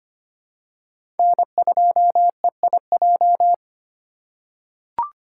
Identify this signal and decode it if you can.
Morse code, heard as a keyed tone, about 25 wpm, 700 Hz. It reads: N2EIJ
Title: USA Call Signs - 5 Characters - 2x Character Spacing - 25wpm